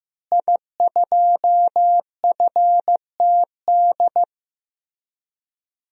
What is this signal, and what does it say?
Morse code, heard as a keyed tone, about 16 wpm, 700 Hz. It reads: I2FTD
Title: DXCC-100 Call Signs - 5 Characters - Repeated 3x - 15wpm